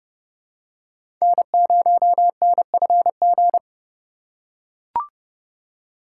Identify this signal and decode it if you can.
Morse code, heard as a keyed tone, about 31 wpm, 700 Hz. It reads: N0NFG
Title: USA Call Signs - 4x Character Spacing - 30wpm